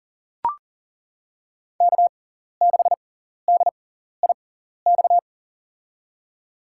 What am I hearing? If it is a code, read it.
Morse code, heard as a keyed tone, about 39 wpm, spaced as for 14 wpm, 700 Hz. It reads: K6DIX
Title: USA Call Signs - 6x Character Spacing - 40wpm